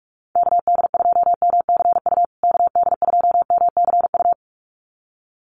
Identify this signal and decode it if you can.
Morse code, heard as a keyed tone, about 44 wpm, 700 Hz. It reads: KD2MCU KD2MCU
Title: USA Call Signs Repeated 2x 45wpm